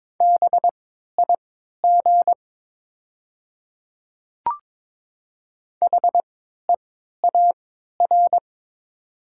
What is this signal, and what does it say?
Morse code, heard as a keyed tone, about 22 wpm, spaced as for 12 wpm, 700 Hz. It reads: BIG HEAR
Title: 3x Character Spacing - Top 300 Words - 22wpm